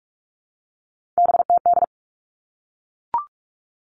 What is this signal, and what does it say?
Morse code, heard as a keyed tone, about 44 wpm, 700 Hz. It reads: BTD
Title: All Cut Numbers - 3-Digit 45wpm